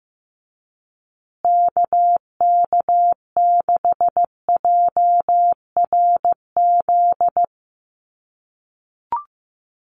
Morse code, heard as a keyed tone, about 15 wpm, 700 Hz. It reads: KK6JRZ
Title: USA Call Signs Repeated 2x 15wpm